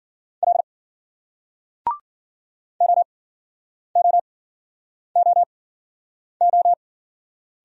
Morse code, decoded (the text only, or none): L OOOO